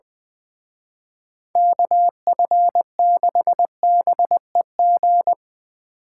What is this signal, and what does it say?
Morse code, heard as a keyed tone, about 22 wpm, 700 Hz. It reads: KF6BEG
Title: Speed Racing - USA 2x3 Call Signs 20wpm